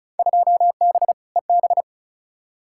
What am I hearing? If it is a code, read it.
Morse code, heard as a keyed tone, about 34 wpm, 700 Hz. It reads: 2B EB
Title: Field Day Exchanges - 35wpm